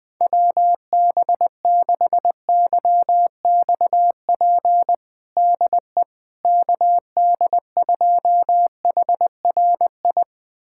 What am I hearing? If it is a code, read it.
Morse code, heard as a keyed tone, about 20 wpm, 700 Hz. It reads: WB6YXP DE KD2HRI
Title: USA Callsign DE Callsign 20wpm